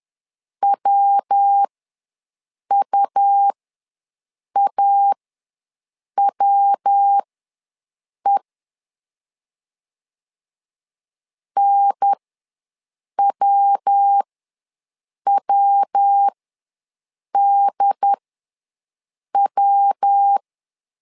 Morse code, heard as a keyed tone, about 10 wpm, spaced as for 6 wpm, 800 Hz. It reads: WUAWE NWWDW
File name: LEZIONI / 3 MWRO / 3 MW ws 10wpm 2213